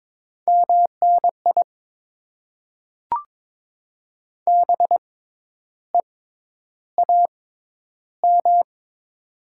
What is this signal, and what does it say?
Morse code, heard as a keyed tone, about 22 wpm, spaced as for 10 wpm, 700 Hz. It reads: MNI BEAM
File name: Top 90 QSO Elements - 6x Character Spacing - 22wpm